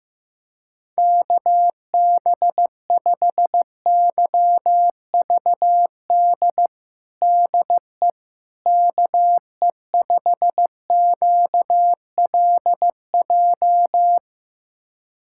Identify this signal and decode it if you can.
Morse code, heard as a keyed tone, about 15 wpm, 700 Hz. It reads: KB5YVD DE KE5QLJ
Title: USA Callsign DE Callsign - 6 Characters - 15wpm